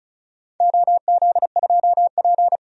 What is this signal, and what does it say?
Morse code, heard as a keyed tone, about 36 wpm, 700 Hz. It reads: OZ2P
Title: DXCC-100 Call Signs - 4 Characters - Repeated 3x - 35wpm